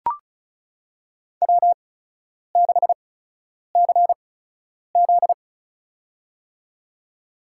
Morse code, heard as a keyed tone, about 34 wpm, spaced as for 10 wpm, 700 Hz. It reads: W6CZ
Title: USA Call Signs - 4 Characters - 8x Character Spacing - 35wpm